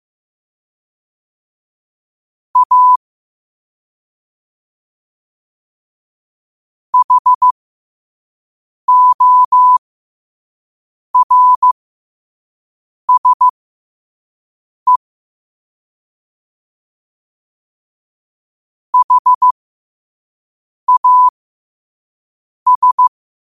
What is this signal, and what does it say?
Morse code, heard as a keyed tone, about 15 wpm, spaced as for 5 wpm, 1000 Hz. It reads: A HORSE HAS
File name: qotdfiles / 5wpm